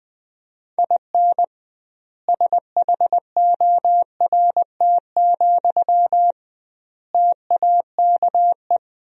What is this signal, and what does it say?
Morse code, heard as a keyed tone, about 21 wpm, 700 Hz. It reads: IN SHORT, TAKE
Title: 2x Word-Spacing - Sentences from Top 400 Words - 20wpm